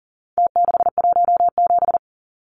Morse code, heard as a keyed tone, about 39 wpm, 700 Hz. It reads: T617
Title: Numbers - Contest Style - 4-Digit 40wpm